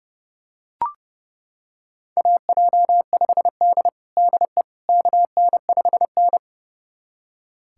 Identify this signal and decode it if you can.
Morse code, heard as a keyed tone, about 32 wpm, 700 Hz. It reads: AJ5D DE KN5N
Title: USA Callsign DE Callsign - 4 Characters - 30wpm